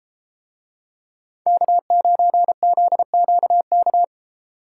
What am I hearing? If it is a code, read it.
Morse code, heard as a keyed tone, about 32 wpm, 700 Hz. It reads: K9ZQK